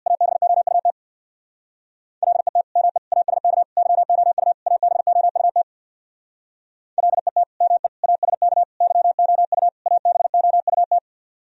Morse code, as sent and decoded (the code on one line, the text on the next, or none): .- -... --- ..- - / .-.. - / -- . / .- ... -.- / -.-- --- ..- / .- -... --- ..- - / .-.. - / -- . / .- ... -.- / -.-- --- ..- / .- -... --- ..- -
ABOUT LET ME ASK YOU ABOUT LET ME ASK YOU ABOUT